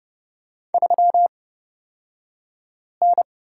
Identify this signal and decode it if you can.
Morse code, heard as a keyed tone, about 30 wpm, spaced as for 5 wpm, 700 Hz. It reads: 3N
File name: Lesson 36d - BT - Two Characters 30wpm